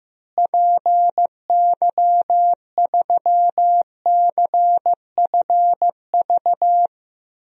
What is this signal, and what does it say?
Morse code, heard as a keyed tone, about 15 wpm, 700 Hz. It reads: PY3CFV
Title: DXCC-100 Call Signs - 6 Characters - Repeated 2x - 15wpm